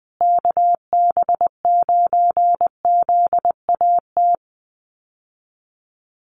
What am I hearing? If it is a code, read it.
Morse code, heard as a keyed tone, about 20 wpm, 700 Hz. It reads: KB9ZAT
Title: USA Call Signs Repeated 3x 20wpm